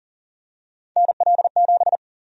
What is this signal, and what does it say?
Morse code, heard as a keyed tone, about 40 wpm, 700 Hz. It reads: NL7
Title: USA Call Sign Prefixes - Repeated 2x - 40wpm